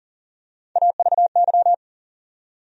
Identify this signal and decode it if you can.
Morse code, heard as a keyed tone, about 40 wpm, 700 Hz. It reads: AVY
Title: USA Call Sign Postfixes - 3 Letters 40wpm